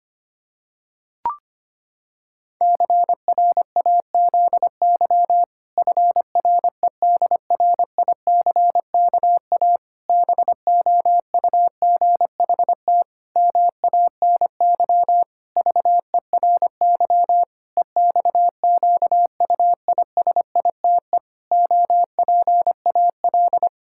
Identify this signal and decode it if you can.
Morse code, heard as a keyed tone, about 25 wpm, 700 Hz. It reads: CRAZY FREDRICKA BOUGHT MANY VERY EXQUISITE OPAL
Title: Pangram Challenge v2 - 25wpm